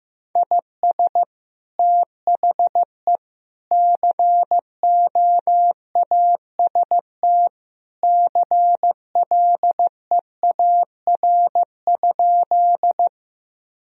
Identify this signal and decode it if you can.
Morse code, heard as a keyed tone, about 15 wpm, 700 Hz. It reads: IS THE COAST CLEAR?
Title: Sentences from Top 1000 Words - Review Part 4 - 15wpm